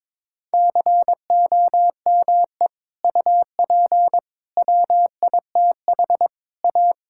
Morse code, heard as a keyed tone, about 21 wpm, 700 Hz. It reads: COME UP WITH A
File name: practice / 22wpm / Sentences with 3-Word Phrases - 22wpm